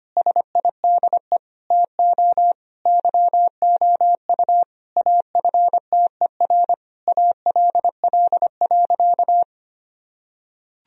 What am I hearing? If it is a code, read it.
Morse code, heard as a keyed tone, about 25 wpm, 700 Hz. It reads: SIDE TO YOU AFTER ALL.
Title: Sentences from Top 1000 Words - Review Part 3 - 25wpm